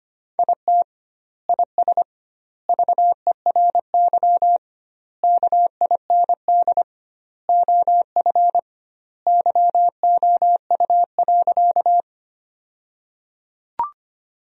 Morse code, heard as a keyed tone, about 25 wpm, 700 Hz. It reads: IT IS VERY KIND OF YOU.
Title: 2x Word-Spacing - Sentences from Top 200 Words - 25wpm